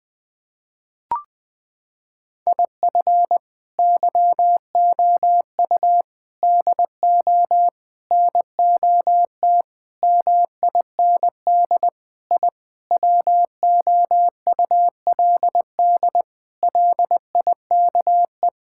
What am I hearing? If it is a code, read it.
Morse code, heard as a keyed tone, about 20 wpm, 700 Hz. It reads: IF YOU DO NOT MIND I WOULD LIKE
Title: Sentences with 5-Word Phrases - 20wpm